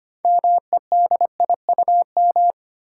Morse code, heard as a keyed tone, about 25 wpm, 700 Hz. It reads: MEDIUM
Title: Six Letter Words - 25wpm